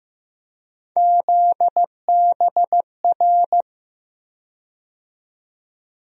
Morse code, heard as a keyed tone, about 15 wpm, 700 Hz. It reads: ZBR
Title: USA Call Sign Postfixes - 3 Letters 15wpm